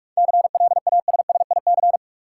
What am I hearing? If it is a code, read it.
Morse code, heard as a keyed tone, about 43 wpm, 700 Hz. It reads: CLASSIC
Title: Sentences from Top 1000 Words - New Word Drill - 45wpm